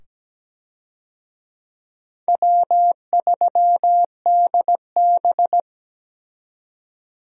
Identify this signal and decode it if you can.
Morse code, heard as a keyed tone, about 17 wpm, 700 Hz. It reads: W3DB
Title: USA Call Signs - 4 Characters - 17wpm